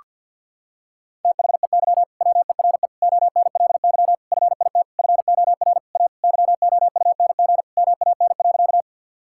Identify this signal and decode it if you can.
Morse code, heard as a keyed tone, about 50 wpm, 700 Hz. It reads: THEY WERE ONLY FIT FOR A YOUNG MAN.
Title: Sentences from Top 1000 Words - Review Part 5 - 50wpm